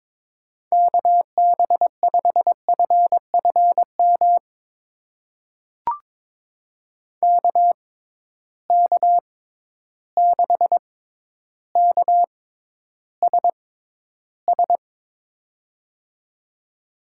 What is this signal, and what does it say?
Morse code, heard as a keyed tone, about 22 wpm, spaced as for 11 wpm, 700 Hz. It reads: KB5FFM KK6KSS
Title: USA Call Signs - 6 Characters - 6x Character Spacing - 22wpm